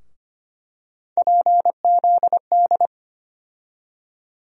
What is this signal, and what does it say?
Morse code, heard as a keyed tone, about 25 wpm, 700 Hz. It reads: PZD
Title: USA Call Sign Postfixes - 3 Letters 25wpm